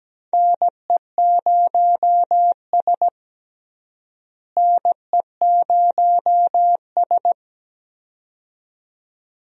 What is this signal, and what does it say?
Morse code, heard as a keyed tone, about 17 wpm, 700 Hz. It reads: NE0S NE0S
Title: USA Call Signs - 4 Characters - Repeated 2x - 17wpm